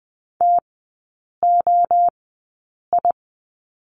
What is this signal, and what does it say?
Morse code, heard as a keyed tone, about 20 wpm, spaced as for 8 wpm, 690 Hz. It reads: TOI
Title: 145 Lesson 24e - Letter y - Three Characters 20wpm